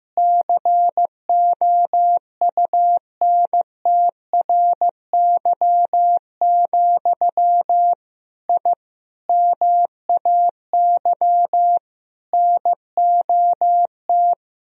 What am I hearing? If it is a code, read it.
Morse code, heard as a keyed tone, about 16 wpm, 700 Hz. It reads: COUNTRY, I MAY NOT
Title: Sentences from Top 1000 Words - Review Part 2 - 15wpm